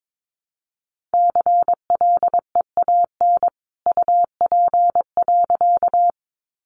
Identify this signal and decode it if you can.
Morse code, heard as a keyed tone, about 22 wpm, 700 Hz. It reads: CLEAN UP.
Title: A Sound Of Thunder 22wpm